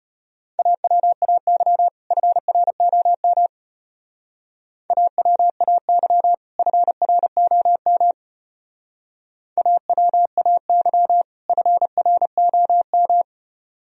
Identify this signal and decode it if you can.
Morse code, heard as a keyed tone, about 31 wpm, 700 Hz. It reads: AWAY FROM AWAY FROM AWAY FROM
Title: Speed-Racing - Sets of 2 Words 25wpm